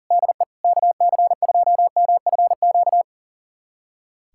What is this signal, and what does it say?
Morse code, heard as a keyed tone, about 40 wpm, 700 Hz. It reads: DE KC2MFQ
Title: USA Callsign DE Callsign 40wpm